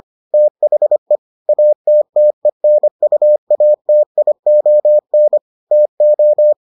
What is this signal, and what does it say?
Morse code, heard as a keyed tone, about 25 wpm, 600 Hz. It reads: THE ATTENUATION TO